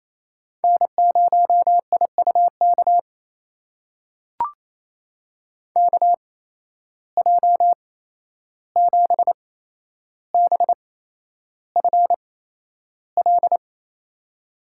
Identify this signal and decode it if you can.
Morse code, heard as a keyed tone, about 28 wpm, spaced as for 11 wpm, 700 Hz. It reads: N0IUK KJ7BFL
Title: USA Call Signs - 8x Character Spacing - 28wpm